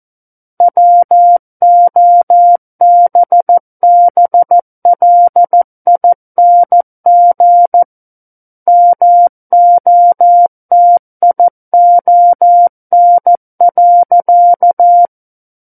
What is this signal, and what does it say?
Morse code, heard as a keyed tone, about 14 wpm, 700 Hz. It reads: WOBBLING MOTION.
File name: Book One / 05-Chapter 5 - 14 WPM